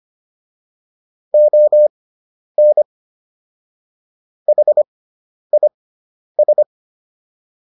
Morse code, heard as a keyed tone, about 25 wpm, spaced as for 10 wpm, 600 Hz. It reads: ON HIS